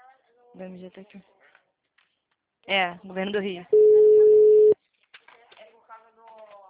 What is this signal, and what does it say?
Ringback tone (Europe, 425 Hz), with speech.